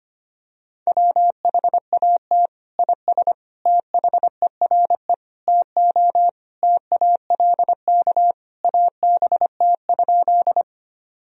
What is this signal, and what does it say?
Morse code, heard as a keyed tone, about 25 wpm, 700 Hz. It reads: WHAT IS THERE TO TALK ABT?
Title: Sentences from Top 400 Words and Abbreviations 25wpm